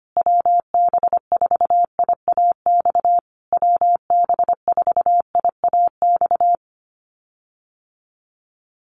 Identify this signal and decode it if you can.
Morse code, heard as a keyed tone, about 25 wpm, 700 Hz. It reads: WB4IAX WB4IAX
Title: USA Call Signs Repeated 3x 25wpm